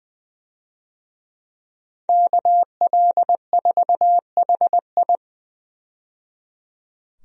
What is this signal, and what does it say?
Morse code, heard as a keyed tone, about 20 wpm, 700 Hz. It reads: KL4HI
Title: USA Call Signs - 5 Characters - 20wpm